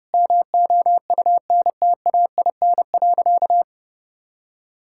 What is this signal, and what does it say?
Morse code, heard as a keyed tone, about 30 wpm, 700 Hz. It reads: MOUNTAIN.
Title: Sentences from Top 1000 Words - Review Part 6 - 30wpm